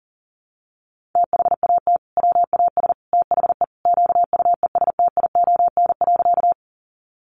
Morse code, heard as a keyed tone, about 40 wpm, 700 Hz. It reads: THAT WAS THE QUESTION.
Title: Sentences from Top 400 Words 40wpm